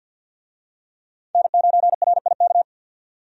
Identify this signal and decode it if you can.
Morse code, heard as a keyed tone, about 50 wpm, 700 Hz. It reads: N9RIK
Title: USA Call Signs - 5 Characters - 50wpm